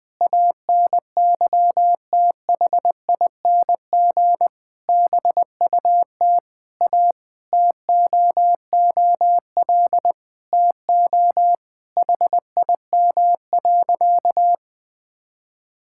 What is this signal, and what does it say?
Morse code, heard as a keyed tone, about 20 wpm, 700 Hz. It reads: ANYTHING BUT A TOOL TO HIM.
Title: Sentences from Top 1000 Words - Review Part 6 - 20wpm